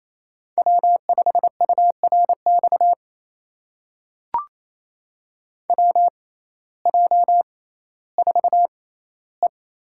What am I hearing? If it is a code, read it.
Morse code, heard as a keyed tone, about 28 wpm, spaced as for 15 wpm, 700 Hz. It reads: W5URX WJ4E
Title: USA Call Signs - 5 Characters - 6x Character Spacing - 28wpm